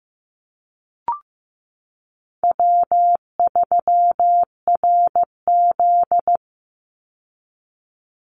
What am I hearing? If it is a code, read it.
Morse code, heard as a keyed tone, about 15 wpm, 700 Hz. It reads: W3RZ
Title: Easy Call Signs Repeated 3x - 15wpm